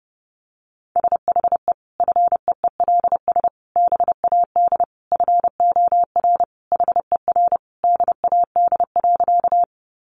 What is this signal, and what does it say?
Morse code, heard as a keyed tone, about 30 wpm, 700 Hz. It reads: SHE FEELS BAD FOR HER DAD.
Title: Additional Easy Phrases 30wpm